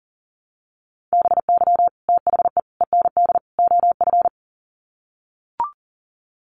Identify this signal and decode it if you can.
Morse code, heard as a keyed tone, about 40 wpm, 700 Hz. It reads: BY THE END OF
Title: Sets of 4 Words 40wpm